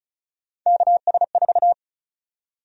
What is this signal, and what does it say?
Morse code, heard as a keyed tone, about 36 wpm, 700 Hz. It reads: KS4
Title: USA Call Sign Prefixes - Repeated 2x - 35wpm